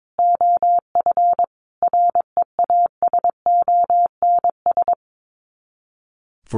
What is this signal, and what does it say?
Morse code, heard as a keyed tone, about 22 wpm, 700 Hz. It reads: OF REASONS
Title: Sets of 5 Words Encore 22wpm